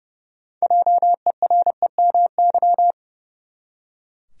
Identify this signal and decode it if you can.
Morse code, heard as a keyed tone, about 29 wpm, 700 Hz. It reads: JEREMY
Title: USA Common Names - 30wpm